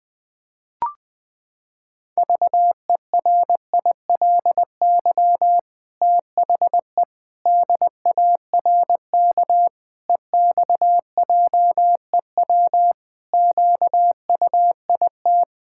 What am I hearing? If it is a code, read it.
Morse code, heard as a keyed tone, about 21 wpm, 700 Hz. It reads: VERILY THE DARK EXJEW QUIT
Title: Pangram Challenge v2 - 20wpm